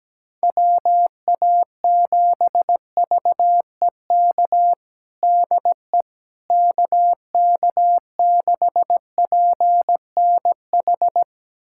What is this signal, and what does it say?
Morse code, heard as a keyed tone, about 17 wpm, 700 Hz. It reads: WA7VEK DE KK6PNH
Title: USA Callsign DE Callsign - 6 Characters - 17wpm